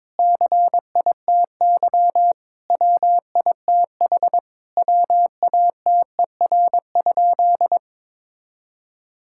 Morse code, heard as a keyed tone, about 22 wpm, 700 Hz. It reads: CITY WITH WATER?
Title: Sentences from Top 1000 Words - Review Part 5 - 22wpm